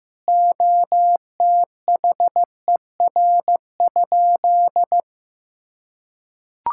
Morse code, heard as a keyed tone, about 15 wpm, 700 Hz. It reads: OTHER?
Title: Sentences from Top 1600 Words 15wpm